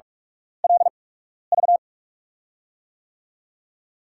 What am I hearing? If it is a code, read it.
Morse code, heard as a keyed tone, about 47 wpm, spaced as for 12 wpm, 700 Hz. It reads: LV
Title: USA Call Signs - 6 Characters - 8x Character Spacing - 45wpm